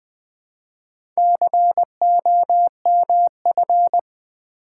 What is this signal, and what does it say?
Morse code, heard as a keyed tone, about 20 wpm, 700 Hz. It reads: COMF
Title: Build A Word - 4 Letters At A Time - 11-Letter Words - 20wpm